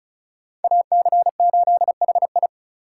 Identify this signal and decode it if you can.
Morse code, heard as a keyed tone, about 36 wpm, 700 Hz. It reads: AC8HI
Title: USA Call Signs - 5 Characters - 4x Character Spacing - 35wpm